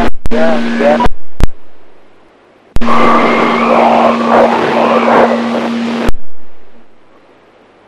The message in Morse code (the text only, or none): none